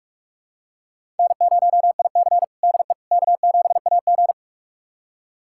Morse code, heard as a keyed tone, about 46 wpm, 700 Hz. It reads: N0IC DE K7AG
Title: USA Callsign DE Callsign - 4 Characters - 45wpm